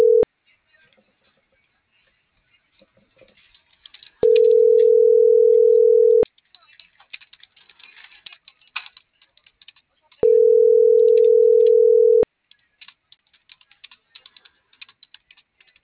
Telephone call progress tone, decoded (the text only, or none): ringback tone